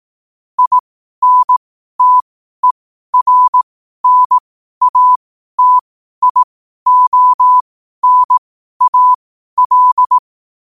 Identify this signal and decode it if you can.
Morse code, heard as a keyed tone, about 18 wpm, spaced as for 12 wpm, 1000 Hz. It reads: INTERNATIONAL
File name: qotdfiles / 12wpm